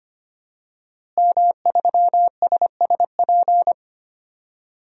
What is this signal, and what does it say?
Morse code, heard as a keyed tone, about 25 wpm, 700 Hz. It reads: M3SSP
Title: DXCC-100 Call Signs - 5 Characters - Repeated 3x - 25wpm